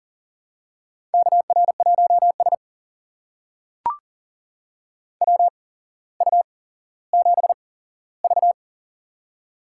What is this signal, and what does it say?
Morse code, heard as a keyed tone, about 38 wpm, spaced as for 17 wpm, 700 Hz. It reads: KR1S WU7V
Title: USA Call Signs - 4 Characters - 8x Character Spacing - 40wpm